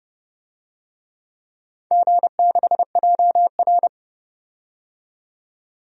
Morse code, heard as a keyed tone, about 30 wpm, 700 Hz. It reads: G6JR
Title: DXCC-100 Call Signs - 4 Characters - 30wpm